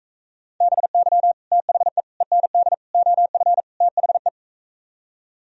Morse code, heard as a keyed tone, about 45 wpm, 700 Hz. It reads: BY THE END OF THE